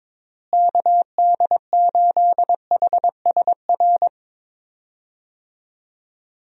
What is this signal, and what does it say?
Morse code, heard as a keyed tone, about 22 wpm, 700 Hz. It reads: KD8HSR
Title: USA Call Signs - 6 Characters - Repeated 2x - 22wpm